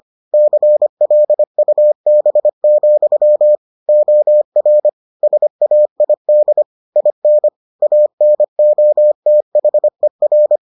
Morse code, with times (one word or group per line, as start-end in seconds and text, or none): 0.3-3.5 CLUB,
3.9-4.9 OR
5.2-6.6 SAID
7.0-7.5 IN
7.8-10.6 ANOTHER